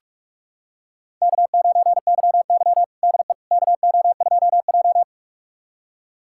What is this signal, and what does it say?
Morse code, heard as a keyed tone, about 46 wpm, 700 Hz. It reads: K9YY DE KO2J
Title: USA Callsign DE Callsign - 4 Characters - 45wpm